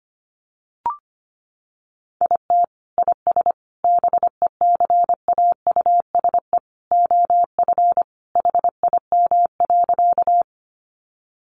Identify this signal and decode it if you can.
Morse code, heard as a keyed tone, about 25 wpm, 700 Hz. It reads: IT IS BECAUSE OF HIM.
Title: Sentences from Top 600 Words 25wpm